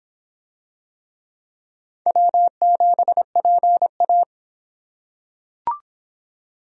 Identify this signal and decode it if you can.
Morse code, heard as a keyed tone, about 24 wpm, 700 Hz. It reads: W7PA